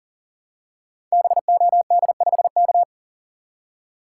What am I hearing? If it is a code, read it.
Morse code, heard as a keyed tone, about 40 wpm, 700 Hz. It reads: BOD5K
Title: Code Groups - 5 Letter:Numbers - 40wpm